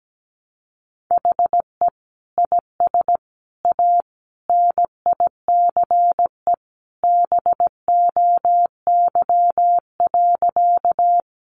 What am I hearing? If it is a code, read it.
Morse code, heard as a keyed tone, about 17 wpm, 700 Hz. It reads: HE IS A NICE BOY.
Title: Sentences from Top 1100 Words 17wpm